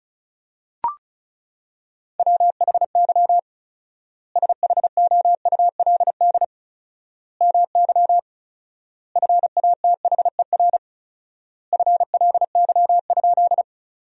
Morse code, heard as a keyed tone, about 36 wpm, 700 Hz. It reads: WHY SHOULD MY FATHER FLY?
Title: 4x Word-Spacing - Sentences from Top 500 Words - 35wpm